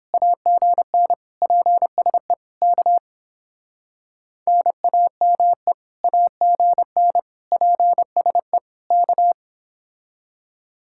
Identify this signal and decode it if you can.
Morse code, heard as a keyed tone, about 30 wpm, 700 Hz. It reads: AGN PSE K NAME AGN PSE K